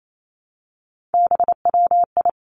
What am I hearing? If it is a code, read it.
Morse code, heard as a keyed tone, about 28 wpm, 700 Hz. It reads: BWI
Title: Three Challenging Letter-Number Combinations 28wpm